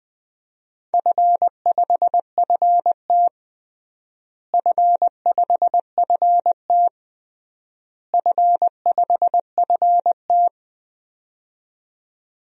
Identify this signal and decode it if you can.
Morse code, heard as a keyed tone, about 20 wpm, 700 Hz. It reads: F5FT F5FT F5FT
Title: DXCC-100 Call Signs - 4 Characters - Repeated 3x - 20wpm